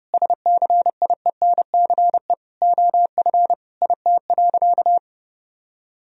Morse code, heard as a keyed tone, about 30 wpm, 700 Hz. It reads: SCIENCE OF IT.
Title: Sentences from Top 1000 Words - Review Part 2 - 30wpm